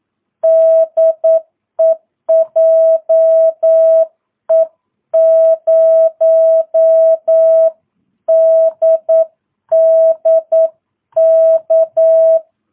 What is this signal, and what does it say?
Morse code, heard as a keyed tone, about 9 wpm, 650 Hz. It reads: DE JE0DDK
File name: jh8cbh / 2022cw